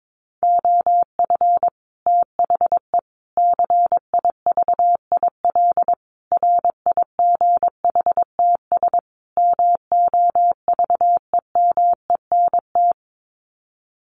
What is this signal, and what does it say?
Morse code, heard as a keyed tone, about 22 wpm, 700 Hz. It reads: OF THE CIVIL RIGHTS MOVEMENT
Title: Sets of 5 Words Encore2 22wpm